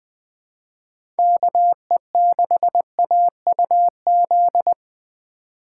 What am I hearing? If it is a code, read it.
Morse code, heard as a keyed tone, about 20 wpm, 700 Hz. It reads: KE6AUZ